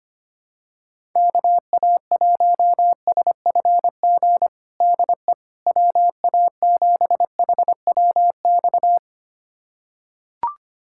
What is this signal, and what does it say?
Morse code, heard as a keyed tone, about 25 wpm, 700 Hz. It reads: KA1SFG DE WA7HWX
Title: USA Callsign DE Callsign 25wpm